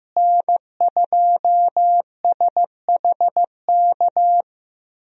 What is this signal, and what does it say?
Morse code, heard as a keyed tone, about 15 wpm, 700 Hz. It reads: N2SHK
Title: USA Call Signs - 5 Characters - Repeated 3x - 15wpm